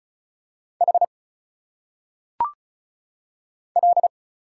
Morse code, heard as a keyed tone, about 34 wpm, 700 Hz. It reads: H L
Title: Single Letters - Phonetic Pronunciation 35wpm